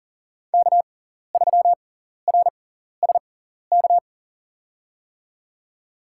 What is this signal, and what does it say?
Morse code, heard as a keyed tone, about 37 wpm, spaced as for 14 wpm, 700 Hz. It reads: K3RSK
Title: USA Call Signs - 5 Characters - 6x Character Spacing - 40wpm